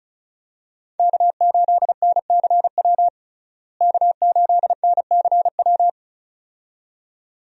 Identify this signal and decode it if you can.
Morse code, heard as a keyed tone, about 36 wpm, 700 Hz. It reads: K8NCW K8NCW
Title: USA Call Signs - 5 Characters - Repeated 2x - 35wpm